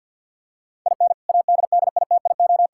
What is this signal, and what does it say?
Morse code, heard as a keyed tone, about 53 wpm, 700 Hz. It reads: IN ADDITIO
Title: Sentences from Top 1000 Words - Review Part 6 - 50wpm